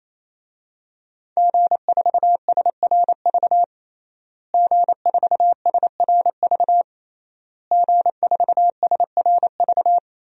Morse code, heard as a keyed tone, about 28 wpm, 700 Hz. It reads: G4SRV G4SRV G4SRV
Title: DXCC-100 Call Signs - 5 Characters - Repeated 3x - 28wpm